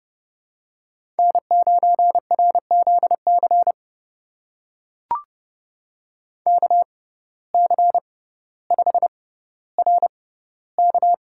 Morse code, heard as a keyed tone, about 30 wpm, spaced as for 15 wpm, 700 Hz. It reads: N9RZC KC5RK
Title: USA Call Signs - 5 Characters - 6x Character Spacing - 30wpm